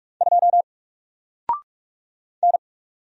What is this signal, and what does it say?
Morse code, heard as a keyed tone, about 44 wpm, 700 Hz. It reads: 2 N
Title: Single Letter-Number 45wpm